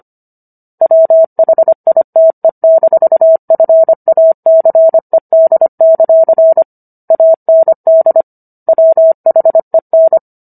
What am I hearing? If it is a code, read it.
Morse code, heard as a keyed tone, about 25 wpm, 650 Hz. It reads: WHITE-FACED; AND WHEN